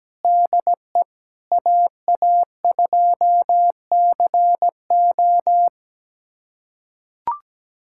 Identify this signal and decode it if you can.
Morse code, heard as a keyed tone, about 17 wpm, 700 Hz. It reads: DE AA2CO
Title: USA Callsign DE Callsign - 5 Characters - 17wpm